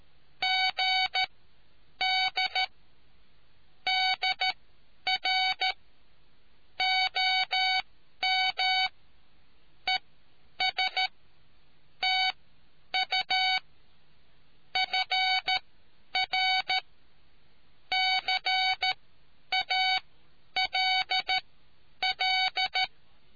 Morse code, heard as a keyed tone, about 13 wpm, spaced as for 9 wpm, 750 Hz. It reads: GD DR OM ES TU FR CALL